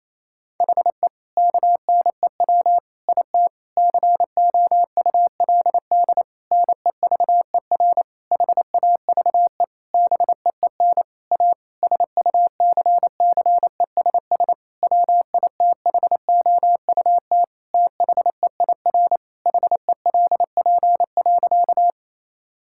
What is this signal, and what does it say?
Morse code, heard as a keyed tone, about 28 wpm, 700 Hz. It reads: HE KNEW IT COULD NEVER HAVE BEEN A SUCCESS WITHOUT THEIR HELP.
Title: Sentences from Top 1000 Words - Review Part 6 - 28wpm